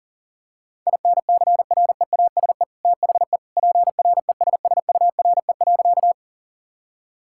Morse code, heard as a keyed tone, about 40 wpm, 700 Hz. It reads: INCREASE THE PRESSURE.
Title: Sentences from Top 1000 Words - Review Part 6 - 40wpm